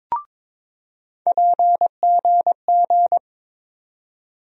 Morse code, heard as a keyed tone, about 22 wpm, 700 Hz. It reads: PGG
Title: USA Call Sign Postfixes - 3 Letters 22wpm